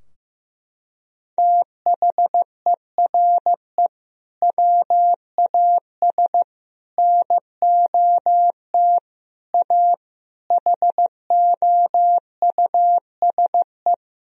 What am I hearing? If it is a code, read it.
Morse code, heard as a keyed tone, about 15 wpm, 700 Hz. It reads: THERE WAS NOT A HOUSE
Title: Sentences from Top 1000 Words - Review Part 2 - 15wpm